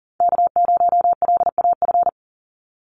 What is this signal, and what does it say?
Morse code, heard as a keyed tone, about 40 wpm, 700 Hz. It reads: K0LAF
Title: NA SOTA Chaser Callsigns 40wpm